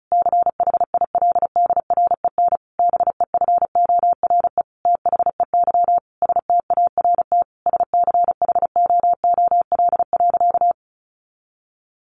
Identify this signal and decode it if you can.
Morse code, heard as a keyed tone, about 34 wpm, 700 Hz. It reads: CHILDREN BEFORE THEY START SCHOOL.
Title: Sentences from Top 1100 Words 35wpm